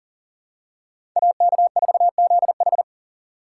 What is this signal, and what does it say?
Morse code, heard as a keyed tone, about 40 wpm, 700 Hz. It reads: AK4ZH
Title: USA Call Signs - 3x Character Spacing - 40wpm